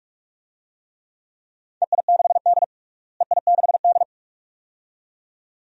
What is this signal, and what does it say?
Morse code, heard as a keyed tone, about 46 wpm, 700 Hz. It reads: EI6D EI6D
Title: DXCC-100 Call Signs - 4 Characters - Repeated 2x - 45wpm